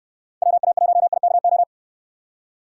Morse code, heard as a keyed tone, about 69 wpm, 700 Hz. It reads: WA1IKO